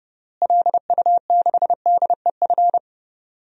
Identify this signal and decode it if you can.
Morse code, heard as a keyed tone, about 30 wpm, 700 Hz. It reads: LU6DEF
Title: DXCC-100 Call Signs - 6 Characters - Repeated 2x - 30wpm